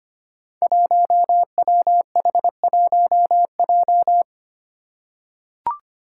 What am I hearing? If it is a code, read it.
Morse code, heard as a keyed tone, about 25 wpm, 700 Hz. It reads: 1WH1J
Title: Five Challenging Letter-Number Combinations 25wpm